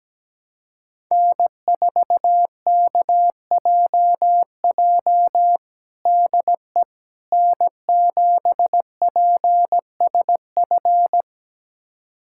Morse code, heard as a keyed tone, about 17 wpm, 700 Hz. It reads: N4KJJ DE N7PSF
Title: USA Callsign DE Callsign - 5 Characters - 17wpm